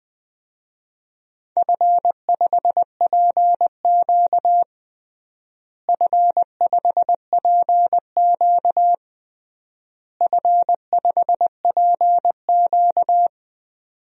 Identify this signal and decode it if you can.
Morse code, heard as a keyed tone, about 21 wpm, 700 Hz. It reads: F5PQ F5PQ F5PQ
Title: DXCC-100 Call Signs - 4 Characters - Repeated 3x - 20wpm